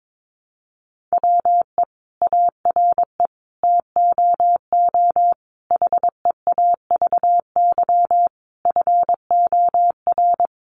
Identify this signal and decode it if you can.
Morse code, heard as a keyed tone, about 22 wpm, 700 Hz. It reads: WE ARE TOO HEAVY FOR
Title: Sentences from Top 1100 Words 22wpm